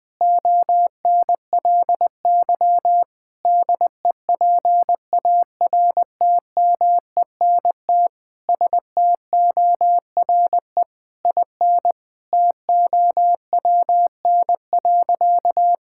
Morse code, heard as a keyed tone, about 20 wpm, 700 Hz. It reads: ONLY DEPARTMENT STORE IN TOWN.
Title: Sentences from Top 1000 Words - Review Part 5 - 20wpm